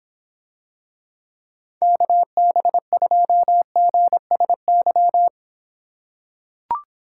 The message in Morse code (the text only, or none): KB2GSY